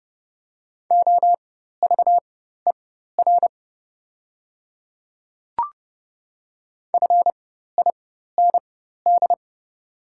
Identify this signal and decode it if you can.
Morse code, heard as a keyed tone, about 30 wpm, spaced as for 14 wpm, 700 Hz. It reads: OVER FIND
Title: 4x Character Spacing - Top 100 Words - 30wpm